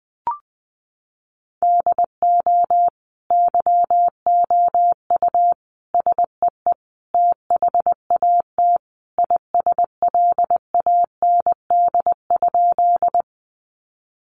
Morse code, heard as a keyed tone, about 19 wpm, 700 Hz. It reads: DO YOU SEE THAT ISLAND?
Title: Sentences from Top 500 Words 20wpm